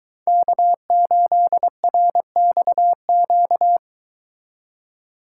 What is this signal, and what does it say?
Morse code, heard as a keyed tone, about 23 wpm, 700 Hz. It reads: K8RXQ